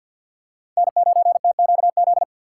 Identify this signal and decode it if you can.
Morse code, heard as a keyed tone, about 53 wpm, 700 Hz. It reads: N9TQZ
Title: USA Call Signs - 5 Characters - 4x Character Spacing - 50wpm